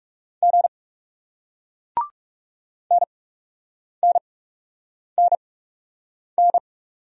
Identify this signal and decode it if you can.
Morse code, heard as a keyed tone, about 41 wpm, spaced as for 8 wpm, 700 Hz. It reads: G NNNN